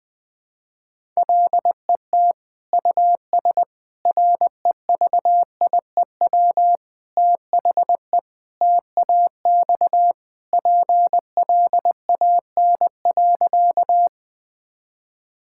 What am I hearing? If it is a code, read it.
Morse code, heard as a keyed tone, about 20 wpm, 700 Hz. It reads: LET US REVIEW THE TAX PLAN.
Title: Sentences from Top 1000 Words - Review Part 6 - 20wpm